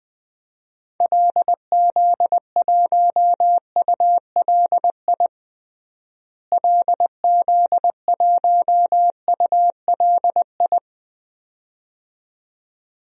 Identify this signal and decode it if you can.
Morse code, heard as a keyed tone, about 20 wpm, 700 Hz. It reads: LZ1ULI LZ1ULI
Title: DXCC-100 Call Signs - 6 Characters - Repeated 2x - 20wpm